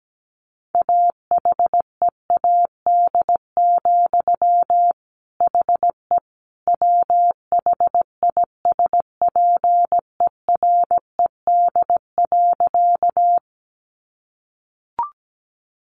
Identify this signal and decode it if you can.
Morse code, heard as a keyed tone, about 17 wpm, 700 Hz. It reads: AHEAD, HE WHISPERED.
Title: A Sound Of Thunder 17wpm